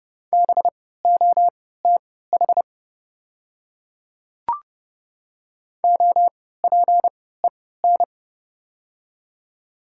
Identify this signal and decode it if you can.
Morse code, heard as a keyed tone, about 30 wpm, spaced as for 17 wpm, 690 Hz. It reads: BOTH OPEN
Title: 3x Character Spacing - Top 500 Words - 30wpm